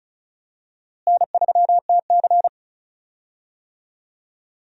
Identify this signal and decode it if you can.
Morse code, heard as a keyed tone, about 34 wpm, 700 Hz. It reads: N3TC
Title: USA Call Signs - 4 Characters - Repeated 3x - 35wpm